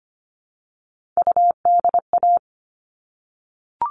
Morse code, heard as a keyed tone, about 25 wpm, 700 Hz. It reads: UDA
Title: All Cut Numbers - 3-Digit 25wpm